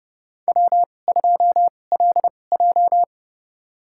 Morse code, heard as a keyed tone, about 29 wpm, spaced as for 21 wpm, 700 Hz. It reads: W2LJ
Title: USA Call Signs - 4 Characters - 2x Character Spacing - 30wpm